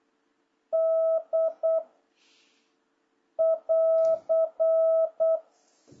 Morse code, heard as a keyed tone, about 8 wpm, 650 Hz. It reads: D <AR>
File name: jh8cbh / 2026cw